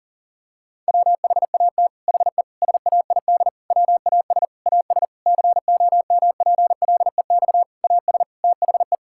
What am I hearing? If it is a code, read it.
Morse code, heard as a keyed tone, about 37 wpm, 700 Hz. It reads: WHAT HE SAID WAS AS COMPLEX AS THE
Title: Sentences from Top 1000 Words - Review Part 6 - 40wpm